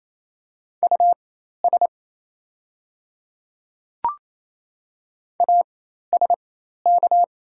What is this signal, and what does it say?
Morse code, heard as a keyed tone, about 28 wpm, spaced as for 13 wpm, 700 Hz. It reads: US ASK
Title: 4x Character Spacing - Top 300 Words - 28wpm